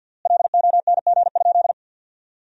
Morse code, heard as a keyed tone, about 50 wpm, 700 Hz. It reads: LONG?
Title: 3x Word-Spacing - Sentences from Top 200 Words - 50wpm